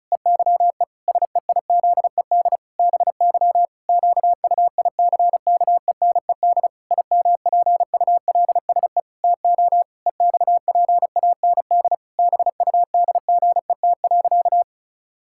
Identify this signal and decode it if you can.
Morse code, heard as a keyed tone, about 34 wpm, 700 Hz. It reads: EYE SEIZED BY QUICKENED IMPULSE TO EXPAND BUDGET.
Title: Pangram Challenge v2 - 35wpm